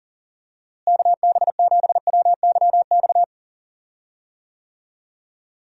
Morse code, heard as a keyed tone, about 40 wpm, 700 Hz. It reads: KB7WYX
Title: USA Call Signs - 6 Characters - Repeated 2x - 40wpm